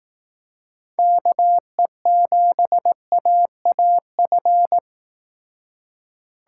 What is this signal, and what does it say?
Morse code, heard as a keyed tone, about 18 wpm, 700 Hz. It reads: KE7AAF
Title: Speed Racing - USA 2x3 Call Signs 15wpm